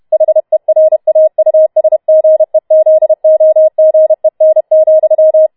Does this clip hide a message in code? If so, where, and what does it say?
Yes: Morse code from 0.1 to 5.5 s, heard as a keyed tone, about 31 wpm, 630 Hz. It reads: HERAUSGEZOGEN,